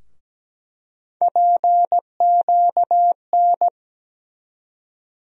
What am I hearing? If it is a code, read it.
Morse code, heard as a keyed tone, about 17 wpm, 700 Hz. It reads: PQN
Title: USA Call Sign Postfixes - 3 Letters 17wpm